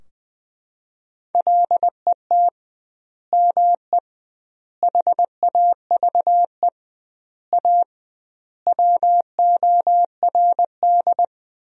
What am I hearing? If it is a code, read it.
Morse code, heard as a keyed tone, about 21 wpm, 700 Hz. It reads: LET ME HAVE A WORD
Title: 2x Word-Spacing - Sentences from Top 300 Words - 20wpm